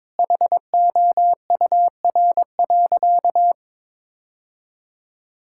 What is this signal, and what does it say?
Morse code, heard as a keyed tone, about 21 wpm, 700 Hz. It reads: HOUR.
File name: practice / 22wpm / Sentences with 3-Word Phrases - 22wpm